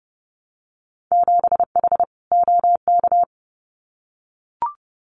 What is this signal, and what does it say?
Morse code, heard as a keyed tone, about 30 wpm, 700 Hz. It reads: 7H OK
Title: Winter Field Day Exchanges Encore 30wpm